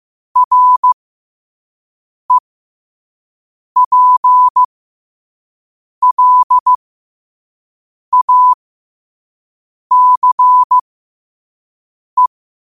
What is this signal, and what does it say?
Morse code, heard as a keyed tone, about 15 wpm, spaced as for 5 wpm, 1000 Hz. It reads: REPLACE
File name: qotdfiles / 5wpm